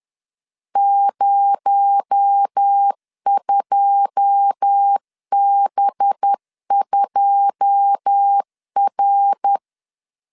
Morse code, heard as a keyed tone, about 10 wpm, 800 Hz. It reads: 02B2R
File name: LEZIONI / 9 2 10wpm 2736